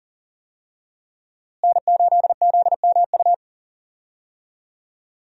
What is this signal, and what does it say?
Morse code, heard as a keyed tone, about 40 wpm, 700 Hz. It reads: N8ZMU
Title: USA Call Signs - 5 Characters - Repeated 3x - 40wpm